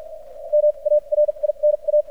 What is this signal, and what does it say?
Morse code, heard as a keyed tone, about 50 wpm, spaced as for 37 wpm, 600 Hz. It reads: MARINA
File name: archived / CRSnet